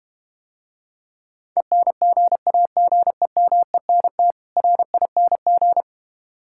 Morse code, heard as a keyed tone, about 32 wpm, 700 Hz. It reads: ENGAGEMENT RING